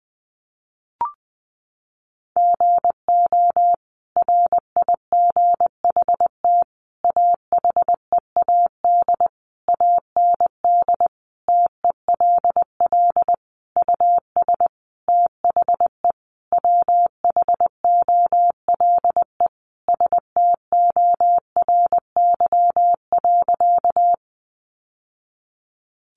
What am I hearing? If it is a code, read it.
Morse code, heard as a keyed tone, about 20 wpm, 700 Hz. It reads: GO RIGHT AHEAD AND TELL US THE WHOLE STORY.
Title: Sentences from Top 1100 Words 20wpm